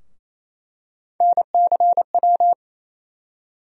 Morse code, heard as a keyed tone, about 27 wpm, 700 Hz. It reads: NCW
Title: USA Call Sign Postfixes - 3 Letters 28wpm